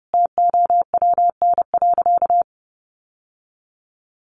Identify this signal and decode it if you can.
Morse code, heard as a keyed tone, about 30 wpm, 700 Hz. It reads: TOWN.
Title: Sentences from Top 1500 Words 30wpm